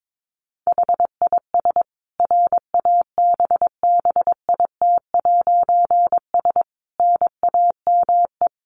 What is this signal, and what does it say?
Morse code, heard as a keyed tone, about 22 wpm, 700 Hz. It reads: HIS RABBIT'S NAME
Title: Easy Phrases 22wpm